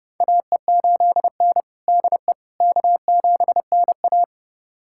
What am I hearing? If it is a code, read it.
Morse code, heard as a keyed tone, about 30 wpm, 700 Hz. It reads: AE8N DE K7NA
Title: USA Callsign DE Callsign - 4 Characters - 30wpm